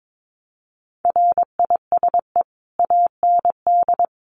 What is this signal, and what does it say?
Morse code, heard as a keyed tone, about 22 wpm, 700 Hz. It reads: RISE AND
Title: Sentences from Top 900 Words 22wpm